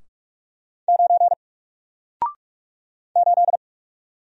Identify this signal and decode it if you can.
Morse code, heard as a keyed tone, about 44 wpm, 700 Hz. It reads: 9 8
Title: Single Letter-Number Alt 45wpm